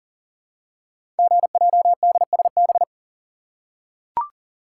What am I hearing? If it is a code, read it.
Morse code, heard as a keyed tone, about 39 wpm, 700 Hz. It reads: GJDSB
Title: Code Groups - 5 Letters - 40wpm